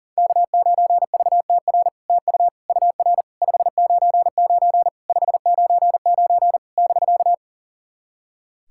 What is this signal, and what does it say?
Morse code, heard as a keyed tone, about 40 wpm, 700 Hz. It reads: K9VTR TU UR 599 599 <BK>
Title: SOTA QSO Practice 40wpm